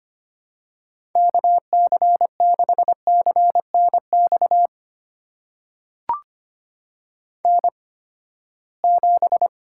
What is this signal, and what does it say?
Morse code, heard as a keyed tone, about 25 wpm, spaced as for 17 wpm, 700 Hz. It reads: KC6CNX N7
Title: USA Call Signs - 8x Character Spacing - 25wpm